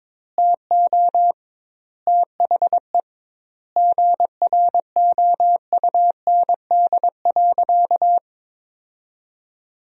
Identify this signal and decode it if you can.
Morse code, heard as a keyed tone, about 22 wpm, 700 Hz. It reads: TO THE GROUND.
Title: 2x Word-Spacing - Sentences from Top 500 Words - 22wpm